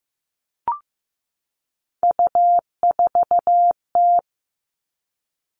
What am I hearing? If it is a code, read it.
Morse code, heard as a keyed tone, about 15 wpm, 700 Hz. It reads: U4T
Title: All Cut Numbers - 3-Digit 15wpm